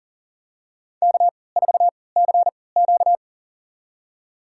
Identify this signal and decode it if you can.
Morse code, heard as a keyed tone, about 40 wpm, spaced as for 23 wpm, 700 Hz. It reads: K4CQ
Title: USA Call Signs - 4 Characters - 3x Character Spacing - 40wpm